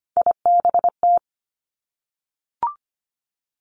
Morse code, heard as a keyed tone, about 25 wpm, 700 Hz. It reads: IBT
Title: Three Challenging Letter-Number Combinations 25wpm